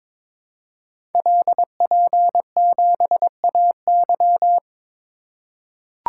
Morse code, heard as a keyed tone, about 22 wpm, 700 Hz. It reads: LP7AY
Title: Code Groups - 5 Letter:Numbers - 22wpm